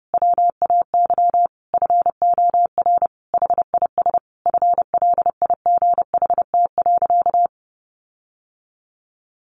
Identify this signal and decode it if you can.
Morse code, heard as a keyed tone, about 30 wpm, 700 Hz. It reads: WAY FOR HIS FLIGHT.
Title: Sentences from Top 1400 Words 30wpm